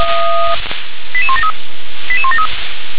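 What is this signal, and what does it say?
Five-tone selective call (ZVEI), heard as quick runs of five tones, tones of about 70 ms. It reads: 88173 88173
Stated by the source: ZVEI five-tone selective call received on SDR, address 88101.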